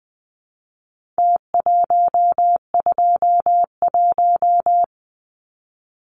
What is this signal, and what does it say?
Morse code, heard as a keyed tone, about 20 wpm, 700 Hz. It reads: T121
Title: Numbers - Contest Style - 4-Digit 20wpm